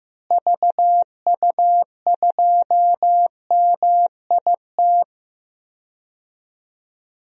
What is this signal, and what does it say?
Morse code, heard as a keyed tone, about 15 wpm, 700 Hz. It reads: VU2MIT
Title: DXCC-100 Call Signs - 6 Characters - Repeated 3x - 15wpm